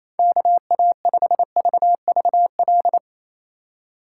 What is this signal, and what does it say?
Morse code, heard as a keyed tone, about 34 wpm, 700 Hz. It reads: KA5VVL
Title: Speed Racing - USA Call Signs 28wpm